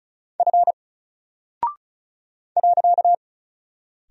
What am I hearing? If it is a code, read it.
Morse code, heard as a keyed tone, about 36 wpm, 700 Hz. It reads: F .
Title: Single Character (Letter:Number:Punctuation:Prosign) 35wpm